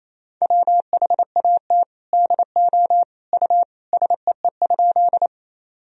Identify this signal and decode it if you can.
Morse code, heard as a keyed tone, about 28 wpm, 700 Hz. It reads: WHAT DO U SEE?
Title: Sentences from Top 100 Words and Abbreviations 28wpm